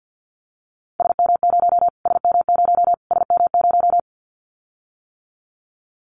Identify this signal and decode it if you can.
Morse code, heard as a keyed tone, about 50 wpm, 700 Hz. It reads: SM0 SM0 SM0
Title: Callsign Prefixes for Top 25 DXCC - Repeated 3x - 50wpm